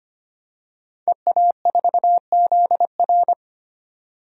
Morse code, heard as a keyed tone, about 25 wpm, 700 Hz. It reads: EA4ZR
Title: DXCC-100 Call Signs - 5 Characters - 25wpm